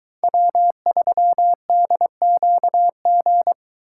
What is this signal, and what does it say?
Morse code, heard as a keyed tone, about 23 wpm, 700 Hz. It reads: W3DQG